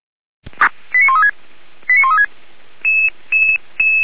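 Five-tone selective call (ZVEI), heard as quick runs of five tones, tones of about 80 ms. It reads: 89136 89136
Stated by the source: Five-tone ZVEI selective call 89135, received by SDR by fire brigade Orlamünde (feuerwehr-orlamuende).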